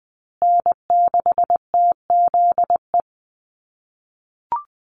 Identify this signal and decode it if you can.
Morse code, heard as a keyed tone, about 20 wpm, 700 Hz. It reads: N6TZE
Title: USA Call Signs Repeated 2x 20wpm